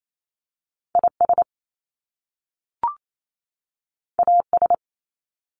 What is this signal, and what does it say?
Morse code, heard as a keyed tone, about 28 wpm, 700 Hz. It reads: I S A S
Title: Top Two Letter Combinations in Words 28wpm